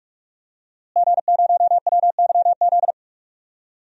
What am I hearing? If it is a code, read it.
Morse code, heard as a keyed tone, about 46 wpm, 700 Hz. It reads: G0WYZ
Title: DXCC-100 Call Signs - 5 Characters - Repeated 3x - 45wpm